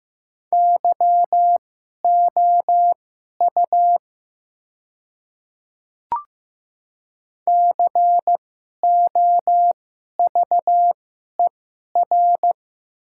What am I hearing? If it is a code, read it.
Morse code, heard as a keyed tone, about 15 wpm, spaced as for 11 wpm, 700 Hz. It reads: YOU COVER
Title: 2x Character Spacing - Top 300 Words - 15wpm